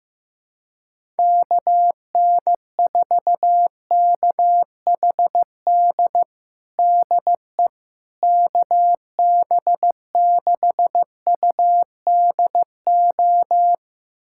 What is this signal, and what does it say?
Morse code, heard as a keyed tone, about 15 wpm, 700 Hz. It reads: KN4KHD DE KB6UDO
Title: USA Callsign DE Callsign 15wpm